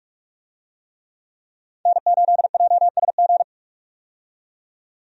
Morse code, heard as a keyed tone, about 48 wpm, 700 Hz. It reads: N8JSG
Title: USA Call Signs - 5 Characters - 45wpm